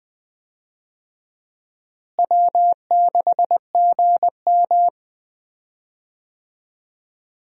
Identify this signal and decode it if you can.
Morse code, heard as a keyed tone, about 20 wpm, 700 Hz. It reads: W6GM
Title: USA Call Signs - 4 Characters - Repeated 2x - 20wpm